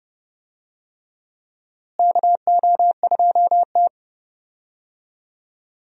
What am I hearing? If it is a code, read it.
Morse code, heard as a keyed tone, about 30 wpm, 700 Hz. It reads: KO2T
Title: USA Call Signs - 4 Characters - Repeated 3x - 30wpm